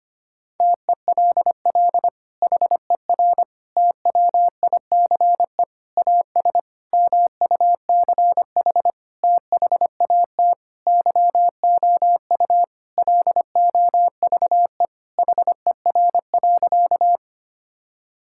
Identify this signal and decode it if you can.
Morse code, heard as a keyed tone, about 25 wpm, 700 Hz. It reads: TELL HER TWICE AS MUCH THAT YOU LOVE HER.
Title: Sentences from Top 1600 Words 25wpm